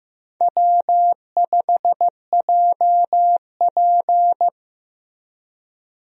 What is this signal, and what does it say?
Morse code, heard as a keyed tone, about 15 wpm, 700 Hz. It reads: W5JP
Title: USA Call Signs - 4 Characters - Repeated 3x - 15wpm